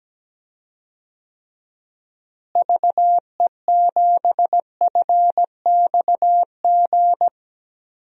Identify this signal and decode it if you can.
Morse code, heard as a keyed tone, about 17 wpm, 700 Hz. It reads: VE7FXG
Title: DXCC-100 Call Signs - 6 Characters - Repeated 2x - 17wpm